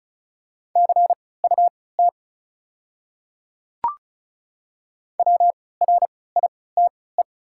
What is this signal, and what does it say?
Morse code, heard as a keyed tone, about 36 wpm, spaced as for 20 wpm, 690 Hz. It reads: CUT WRITE
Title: 3x Character Spacing - Top 300 Words - 35wpm